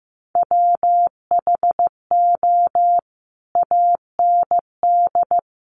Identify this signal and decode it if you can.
Morse code, heard as a keyed tone, about 15 wpm, 700 Hz. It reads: WHO AND
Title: Sentences from Top 100 Words 15wpm